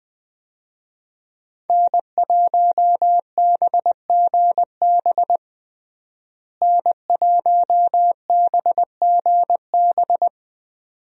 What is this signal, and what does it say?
Morse code, heard as a keyed tone, about 20 wpm, 700 Hz. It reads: N1BGB N1BGB
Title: USA Call Signs - 5 Characters - Repeated 2x - 20wpm